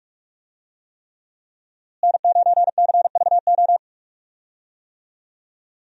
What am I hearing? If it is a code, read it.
Morse code, heard as a keyed tone, about 46 wpm, 700 Hz. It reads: N9CVO
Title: USA Call Signs - 5 Characters - Repeated 3x - 45wpm